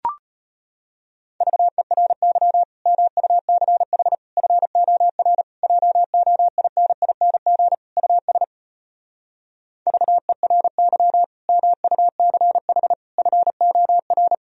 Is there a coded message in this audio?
Yes: Morse code from 1.4 to 14.3 s, heard as a keyed tone, about 33 wpm, 700 Hz. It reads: VERY MUCH FOR JOINING US VERY MUCH FOR